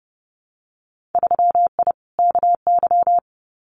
Morse code, heard as a keyed tone, about 30 wpm, 700 Hz. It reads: 3I KY
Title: Winter Field Day Exchanges 30wpm